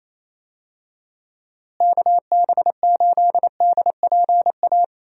Morse code, heard as a keyed tone, about 28 wpm, 700 Hz. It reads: KB8DPA
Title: USA Call Signs - 6 Characters - Repeated 2x - 28wpm